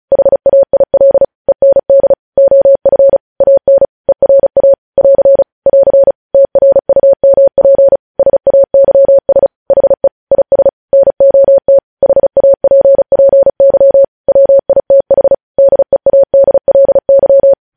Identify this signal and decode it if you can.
Morse code, heard as a keyed tone, about 36 wpm, 550 Hz. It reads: HAIL END OF AN ERA <AR> <AR> TRUMP SAYS HE IS NOT HAPPY WITH DEADLY
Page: News Headlines in Morse Code at 35 WPM